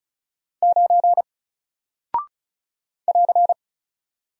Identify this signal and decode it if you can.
Morse code, heard as a keyed tone, about 36 wpm, 700 Hz. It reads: 9 <AR>
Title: Single Character (Letter:Number:Punctuation:Prosign) Alt 35wpm